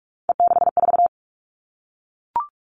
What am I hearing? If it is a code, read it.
Morse code, heard as a keyed tone, about 44 wpm, 700 Hz. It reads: E64
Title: All Cut Numbers - 3-Digit 45wpm